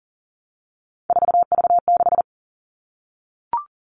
Morse code, heard as a keyed tone, about 40 wpm, 700 Hz. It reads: 4V6
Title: All Cut Numbers - 3-Digit 40wpm